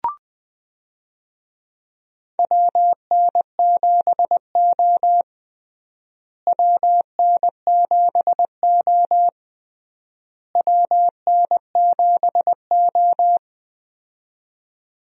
Morse code, heard as a keyed tone, about 21 wpm, 700 Hz. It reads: WN7O WN7O WN7O
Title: USA Call Signs - 4 Characters - Repeated 3x - 20wpm